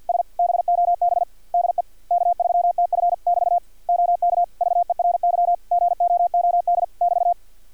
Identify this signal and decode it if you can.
Morse code, heard as a keyed tone, about 51 wpm, 700 Hz, with radio noise. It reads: S BOB DE K3TF <BT> OK VERY GOOD <BT>
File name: archived / CRSnet